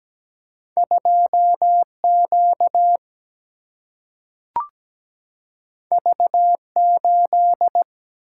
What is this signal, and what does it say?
Morse code, heard as a keyed tone, about 17 wpm, 700 Hz. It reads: 2Q V8
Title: Single-Letter Single-Number 17wpm